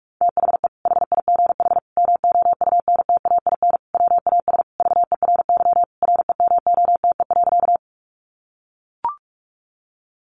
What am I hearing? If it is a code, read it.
Morse code, heard as a keyed tone, about 44 wpm, 700 Hz. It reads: THE HIGH MOUNTAIN WAS VERY REMOTE.
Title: Sentences from Top 1300 Words 45wpm